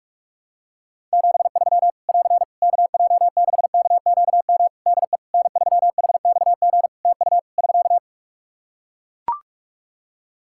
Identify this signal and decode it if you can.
Morse code, heard as a keyed tone, about 43 wpm, 700 Hz. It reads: 73 <AR> KJ6KQM DE N3HXG TU <SK>
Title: Practice QSO - Exchange 4 - Ending - 45wpm